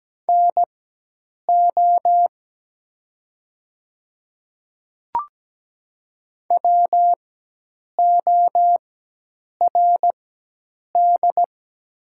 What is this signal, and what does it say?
Morse code, heard as a keyed tone, about 17 wpm, spaced as for 8 wpm, 700 Hz. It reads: NO WORD
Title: 4x Character Spacing - Top 100 Words - 17wpm